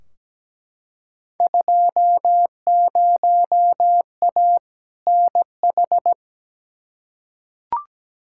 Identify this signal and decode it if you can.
Morse code, heard as a keyed tone, about 17 wpm, 700 Hz. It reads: 20A NH
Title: Field Day Exchanges Encore - 17wpm